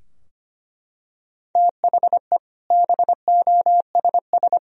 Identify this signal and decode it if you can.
Morse code, heard as a keyed tone, about 25 wpm, 700 Hz. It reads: THE BOSS
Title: Sentences from Top 1000 Words - Review Part 4 - 25wpm